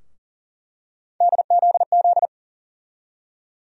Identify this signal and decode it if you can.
Morse code, heard as a keyed tone, about 40 wpm, 700 Hz. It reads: DZZ
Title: USA Call Sign Postfixes - 3 Letters 40wpm